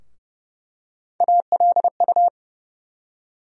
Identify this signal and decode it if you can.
Morse code, heard as a keyed tone, about 32 wpm, 700 Hz. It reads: ALU
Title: USA Call Sign Postfixes - 3 Letters 30wpm